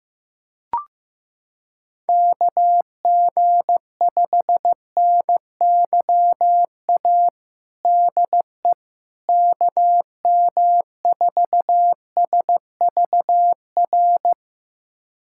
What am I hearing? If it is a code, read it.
Morse code, heard as a keyed tone, about 15 wpm, 700 Hz. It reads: KG5NYA DE KM4SVR
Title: USA Callsign DE Callsign - 6 Characters - 15wpm